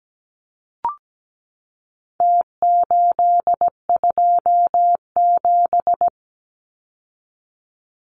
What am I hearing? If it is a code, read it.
Morse code, heard as a keyed tone, about 17 wpm, 700 Hz. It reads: T827
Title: Numbers - Contest Style - 4-Digit 17wpm